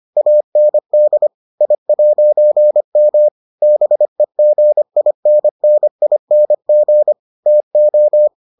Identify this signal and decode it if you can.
Morse code, heard as a keyed tone, about 25 wpm, 600 Hz. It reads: AND I'M BEGINNING TO